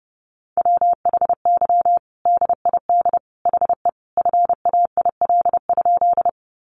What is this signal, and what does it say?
Morse code, heard as a keyed tone, about 30 wpm, 700 Hz. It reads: WHY DID HE FAIL?
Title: Sentences from Top 1100 Words 30wpm